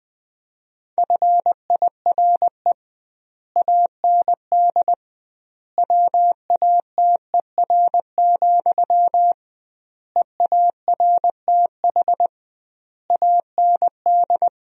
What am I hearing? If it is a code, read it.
Morse code, heard as a keyed tone, about 21 wpm, 700 Hz. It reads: FIRE AND WATER, EARTH AND
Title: 2x Word-Spacing - Sentences from Top 400 Words - 20wpm